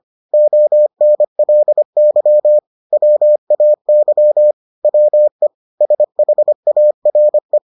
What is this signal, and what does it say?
Morse code, heard as a keyed tone, about 25 wpm, 600 Hz. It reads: ONLY WAY WE SHARE